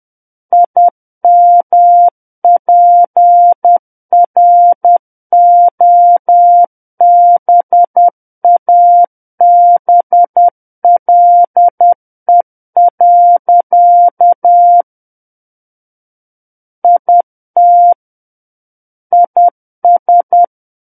Morse code, heard as a keyed tone, about 10 wpm, 700 Hz. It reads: IMPROBABLE. IT IS
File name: Book One / 01-Chapter 1 - 10 WPM